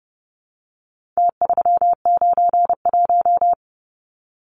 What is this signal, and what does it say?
Morse code, heard as a keyed tone, about 30 wpm, 700 Hz. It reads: T391
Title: Numbers - Contest Style - 4-Digit 30wpm